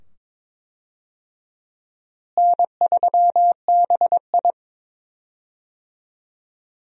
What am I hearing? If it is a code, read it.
Morse code, heard as a keyed tone, about 22 wpm, 700 Hz. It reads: N3BI
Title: USA Call Signs - 4 Characters - Repeated 2x - 22wpm